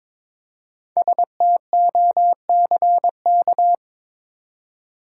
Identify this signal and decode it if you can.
Morse code, heard as a keyed tone, about 22 wpm, 700 Hz. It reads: STOCK
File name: 5 Letter Words - First 3 Letters Given - 22wpm